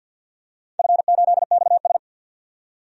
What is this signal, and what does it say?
Morse code, heard as a keyed tone, about 51 wpm, 700 Hz. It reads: F8XS
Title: DXCC-100 Call Signs - 4 Characters - 50wpm